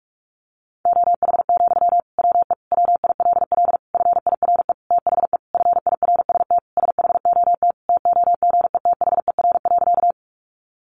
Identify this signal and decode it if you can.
Morse code, heard as a keyed tone, about 44 wpm, 700 Hz. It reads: OH, WE WILL FIRE THE FIRST SHOT TOGETHER.
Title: Sentences from Top 1100 Words 45wpm